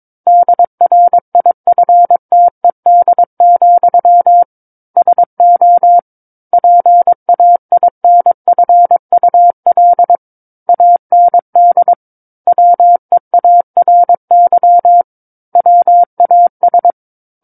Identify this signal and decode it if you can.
Morse code, heard as a keyed tone, about 22 wpm, 700 Hz. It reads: DRIFTED, SO PAINFUL AND WEARY WAS
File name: Book One / 13-Chapter 13 - 22 WPM